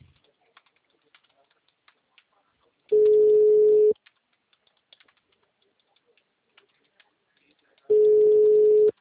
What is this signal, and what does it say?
Ringback tone (Europe, 425 Hz), with speech.
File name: out- -2009- - - .35575